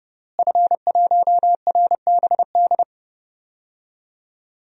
Morse code, heard as a keyed tone, about 30 wpm, 700 Hz. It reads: F1RBD
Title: DXCC-100 Call Signs - 5 Characters - Repeated 2x - 30wpm